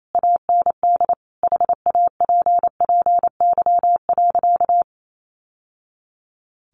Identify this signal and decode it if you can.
Morse code, heard as a keyed tone, about 28 wpm, 700 Hz. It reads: AND HAPPY.
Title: Sentences from Top 1000 Words 28wpm